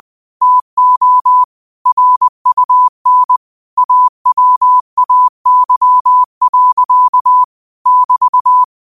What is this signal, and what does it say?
Morse code, heard as a keyed tone, about 20 wpm, 1000 Hz. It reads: TO RUN AWAY. <BT>
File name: qotdfiles / 20wpm